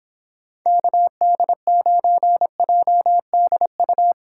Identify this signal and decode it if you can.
Morse code, heard as a keyed tone, about 26 wpm, 700 Hz. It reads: KD9JDU